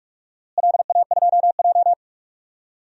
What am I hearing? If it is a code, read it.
Morse code, heard as a keyed tone, about 46 wpm, 700 Hz. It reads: LA2J
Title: DXCC-100 Call Signs - 4 Characters - Repeated 2x - 45wpm